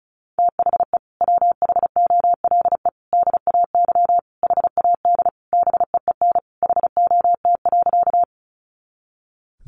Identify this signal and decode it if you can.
Morse code, heard as a keyed tone, about 34 wpm, 700 Hz. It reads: THE WHOLE DAY HAD BEEN HOT.
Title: Sentences from Top 500 Words 35wpm